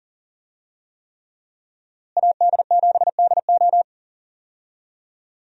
Morse code, heard as a keyed tone, about 37 wpm, 700 Hz. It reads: AD7DO
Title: USA Call Signs - 5 Characters - Repeated 3x - 40wpm